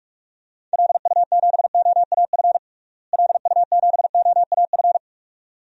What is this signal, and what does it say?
Morse code, heard as a keyed tone, about 43 wpm, 700 Hz. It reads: LU7OAF LU7OAF
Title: DXCC-100 Call Signs - 6 Characters - Repeated 2x - 45wpm